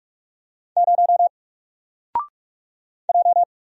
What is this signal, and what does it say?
Morse code, heard as a keyed tone, about 44 wpm, 700 Hz. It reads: 0 J
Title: Single Letter-Number 45wpm